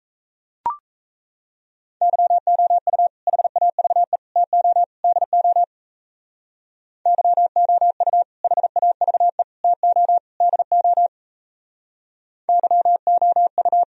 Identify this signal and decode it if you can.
Morse code, heard as a keyed tone, about 35 wpm, 700 Hz. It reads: YOU HAVE TO DO YOU HAVE TO DO YOU
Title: Speed-Racing - Sets of 4 Words 28wpm